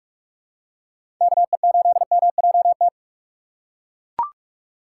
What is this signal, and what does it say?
Morse code, heard as a keyed tone, about 42 wpm, 700 Hz. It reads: KE8MJT
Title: USA Call Signs - 6 Characters - 6x Character Spacing - 45wpm